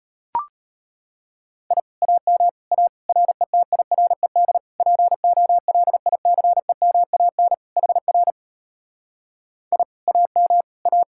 Morse code, heard as a keyed tone, about 33 wpm, 700 Hz. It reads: I AM A RETIRED POLICEMAN HR I AM A